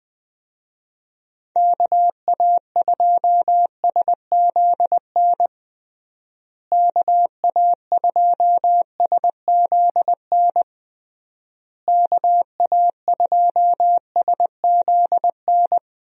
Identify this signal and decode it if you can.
Morse code, heard as a keyed tone, about 21 wpm, 700 Hz. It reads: KA2SZN KA2SZN KA2SZN
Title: USA Call Signs - 6 Characters - Repeated 3x - 20wpm